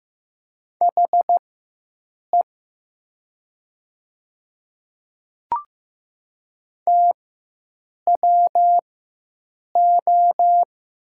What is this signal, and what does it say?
Morse code, heard as a keyed tone, about 16 wpm, spaced as for 7 wpm, 700 Hz. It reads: HE TWO
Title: 4x Character Spacing - Top 300 Words - 15wpm